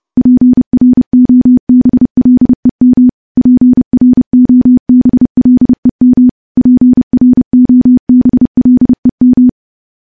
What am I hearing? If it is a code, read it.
Morse code, heard as a keyed tone, about 30 wpm, 250 Hz. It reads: PROBLEM PROBLEM PROBLEM
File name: morse-training-30WPM